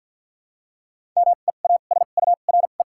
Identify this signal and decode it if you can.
Morse code, heard as a keyed tone, about 53 wpm, 700 Hz. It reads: M E A S U R E
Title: 2x Character Spacing - Top 500 Words - 50wpm